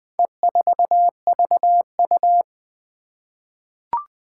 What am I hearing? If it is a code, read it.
Morse code, heard as a keyed tone, about 20 wpm, 700 Hz. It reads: E4VU
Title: Four Challenging Letter-Number Combinations 20wpm